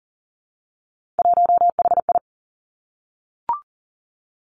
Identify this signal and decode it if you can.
Morse code, heard as a keyed tone, about 38 wpm, 690 Hz. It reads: 1HI
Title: Three Challenging Letter-Number Combinations 40wpm